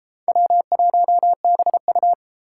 Morse code, heard as a keyed tone, about 32 wpm, 700 Hz. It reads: W1BU